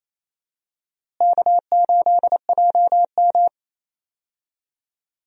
Morse code, heard as a keyed tone, about 28 wpm, 700 Hz. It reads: K8JM
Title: USA Call Signs - 4 Characters - Repeated 3x - 28wpm